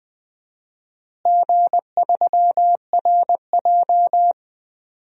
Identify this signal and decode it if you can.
Morse code, heard as a keyed tone, about 20 wpm, 700 Hz. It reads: G3RJ
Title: DXCC-100 Call Signs - 4 Characters - Repeated 3x - 20wpm